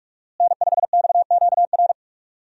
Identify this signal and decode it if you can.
Morse code, heard as a keyed tone, about 46 wpm, 700 Hz. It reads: N5XQR
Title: USA Call Signs - 5 Characters - Repeated 2x - 45wpm